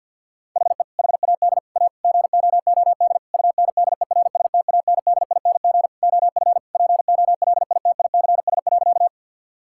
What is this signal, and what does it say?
Morse code, heard as a keyed tone, about 50 wpm, 700 Hz. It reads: HE HAD A GOOD UNDERSTANDING OF POLITICS.
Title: Sentences from Top 1000 Words - Review Part 6 - 50wpm